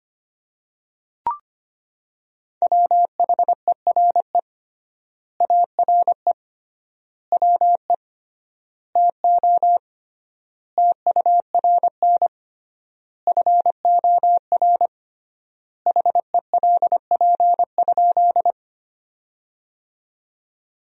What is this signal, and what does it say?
Morse code, heard as a keyed tone, about 25 wpm, 700 Hz. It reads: WHERE ARE WE TO TURN FOR HELP?
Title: 3x Word-Spacing - Sentences from Top 200 Words - 25wpm